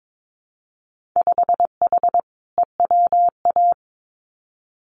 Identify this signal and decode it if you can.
Morse code, heard as a keyed tone, about 22 wpm, 700 Hz. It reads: 5H EWA
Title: Winter Field Day Exchanges Encore 22wpm